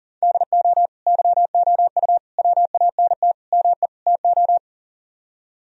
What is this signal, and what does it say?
Morse code, heard as a keyed tone, about 52 wpm, 700 Hz. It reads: DO YOU WANT ME TO